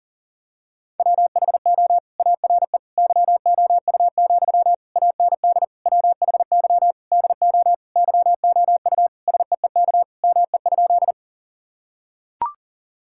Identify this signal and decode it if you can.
Morse code, heard as a keyed tone, about 40 wpm, 700 Hz. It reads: WHO ARE YOU, AND WHY DO YOU SEEK ME?
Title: Sentences from Top 1600 Words 40wpm